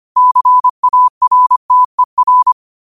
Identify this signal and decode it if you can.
Morse code, heard as a keyed tone, about 25 wpm, 1000 Hz. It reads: CARTER